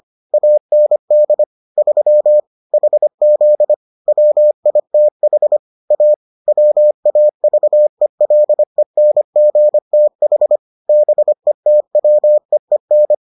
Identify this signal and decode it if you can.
Morse code, heard as a keyed tone, about 25 wpm, 600 Hz. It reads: AND 3 HZ WITH A WAVELENGTH BETWEEN